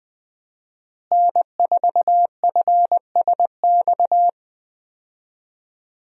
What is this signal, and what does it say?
Morse code, heard as a keyed tone, about 20 wpm, 700 Hz. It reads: N4FSX
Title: USA Call Signs - 5 Characters - Repeated 3x - 20wpm